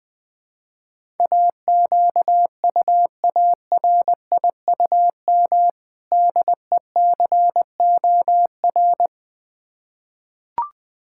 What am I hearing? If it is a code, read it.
Morse code, heard as a keyed tone, about 20 wpm, 690 Hz. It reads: AQUARIUM DECOR
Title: Difficult Words with Hints - 20wpm